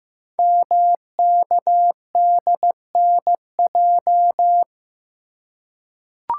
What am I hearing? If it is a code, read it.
Morse code, heard as a keyed tone, about 15 wpm, 700 Hz. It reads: MKDNJ
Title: Code Groups - 5 Letters - 15wpm